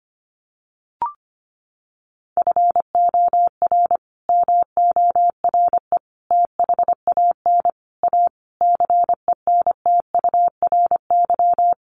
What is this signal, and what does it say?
Morse code, heard as a keyed tone, about 25 wpm, 700 Hz. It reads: FOR MORE THAN A CENTURY
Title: Sets of 5 Words Encore2 25wpm